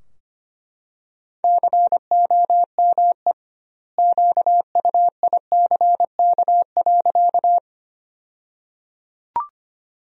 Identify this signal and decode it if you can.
Morse code, heard as a keyed tone, about 25 wpm, 700 Hz. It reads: COME QUICK.
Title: 2x Word-Spacing - Sentences from Top 500 Words - 25wpm